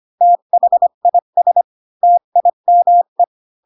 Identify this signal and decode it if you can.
Morse code, heard as a keyed tone, about 25 wpm, 700 Hz. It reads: THIS TIME
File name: Pinocchio / Pinocchio ch1 20 25 ik1hge